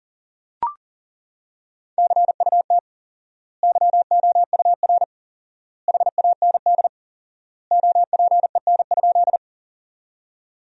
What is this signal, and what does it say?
Morse code, heard as a keyed tone, about 40 wpm, 700 Hz. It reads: CUT YOUR HAND OPEN?
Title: 4x Word-Spacing - Sentences from Top 300 Words - 40wpm